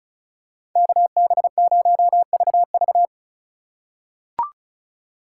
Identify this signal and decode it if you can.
Morse code, heard as a keyed tone, about 34 wpm, 700 Hz. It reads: KB0VV
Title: USA Call Signs - 5 Characters - 4x Character Spacing - 35wpm